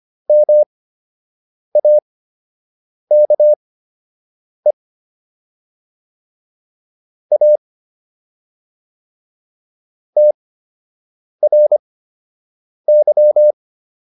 Morse code, heard as a keyed tone, about 25 wpm, spaced as for 7 wpm, 600 Hz. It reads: MAKE A TRY